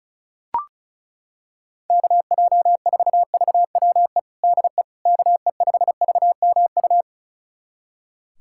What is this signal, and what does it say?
Morse code, heard as a keyed tone, about 36 wpm, 700 Hz. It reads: KJ4VWE DE KE5VMU
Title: USA Callsign DE Callsign - 6 Characters - 35wpm